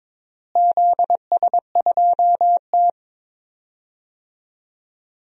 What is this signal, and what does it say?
Morse code, heard as a keyed tone, about 22 wpm, 700 Hz. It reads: ZS2T
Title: DXCC-100 Call Signs - 4 Characters - Repeated 3x - 22wpm